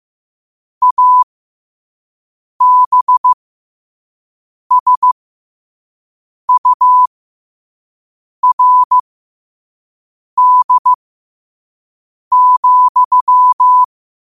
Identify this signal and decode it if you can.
Morse code, heard as a keyed tone, about 15 wpm, spaced as for 5 wpm, 1000 Hz. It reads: ABSURD,
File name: qotdfiles / 5wpm